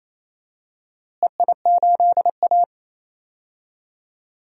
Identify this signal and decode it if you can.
Morse code, heard as a keyed tone, about 27 wpm, 700 Hz. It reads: EI8A
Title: DXCC-100 Call Signs - 4 Characters - Repeated 3x - 28wpm